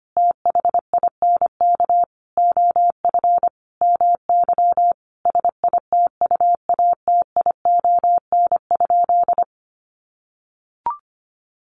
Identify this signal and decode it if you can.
Morse code, heard as a keyed tone, about 25 wpm, 700 Hz. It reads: THINK OF MY SITUATION?
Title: Sentences from Top 1200 Words 25wpm